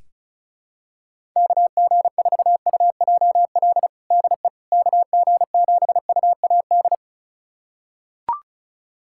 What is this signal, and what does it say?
Morse code, heard as a keyed tone, about 34 wpm, 700 Hz. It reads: KG4UJL DE KG7UAD
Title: USA Callsign DE Callsign 35wpm